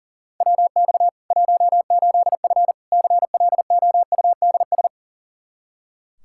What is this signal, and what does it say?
Morse code, heard as a keyed tone, about 37 wpm, 700 Hz. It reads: WX 18F CLOUDS
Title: Practice QSO - Exchange 2 - Info - 40wpm